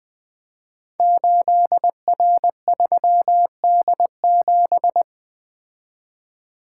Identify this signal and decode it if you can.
Morse code, heard as a keyed tone, about 20 wpm, 700 Hz. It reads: 8R3D7
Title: Code Groups - 5 Letter:Numbers - 20wpm